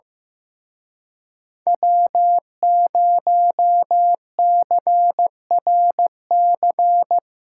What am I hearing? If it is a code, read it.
Morse code, heard as a keyed tone, about 17 wpm, 700 Hz. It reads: W0CRC
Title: Speed Racing - USA 1x3 Call Signs 15wpm